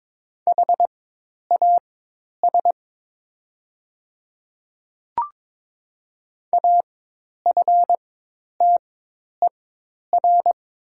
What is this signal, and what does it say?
Morse code, heard as a keyed tone, about 22 wpm, spaced as for 10 wpm, 700 Hz. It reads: HAS AFTER
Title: 4x Character Spacing - Top 300 Words - 22wpm